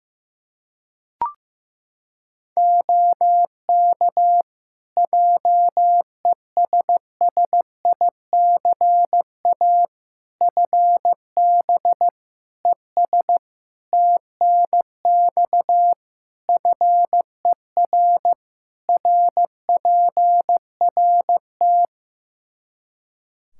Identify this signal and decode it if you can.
Morse code, heard as a keyed tone, about 15 wpm, 700 Hz. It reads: OK JESSICA FB ES TNX FER RPRT
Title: Practice QSO - Exchange 2 - Info - 15wpm